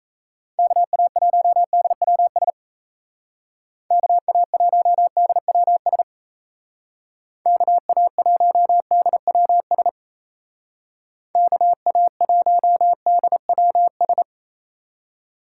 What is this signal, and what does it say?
Morse code, heard as a keyed tone, about 34 wpm, 700 Hz. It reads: KA1DWS KA1DWS KA1DWS KA1DWS
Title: Speed Racing - USA Call Signs 28wpm